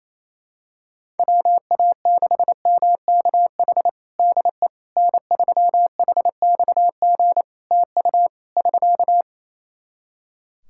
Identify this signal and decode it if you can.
Morse code, heard as a keyed tone, about 28 wpm, 700 Hz. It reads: WA6MKH DE N3HXG TU <SK>
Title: Practice QSO - Exchange 4 - Ending - 28wpm